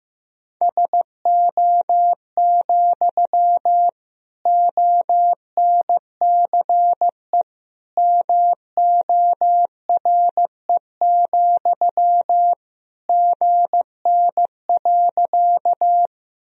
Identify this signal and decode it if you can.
Morse code, heard as a keyed tone, about 15 wpm, 700 Hz. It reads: SO, ONCE MORE, GN.
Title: Sentences from Top 300 Words and Abbreviations 15wpm